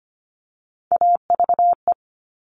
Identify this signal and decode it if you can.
Morse code, heard as a keyed tone, about 25 wpm, 700 Hz. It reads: AVE
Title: All Cut Numbers - 3-Digit 25wpm